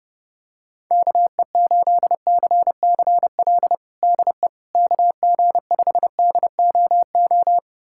Morse code, heard as a keyed tone, about 30 wpm, 700 Hz. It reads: KE8CCL DE KG5DOO
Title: USA Callsign DE Callsign - 6 Characters - 30wpm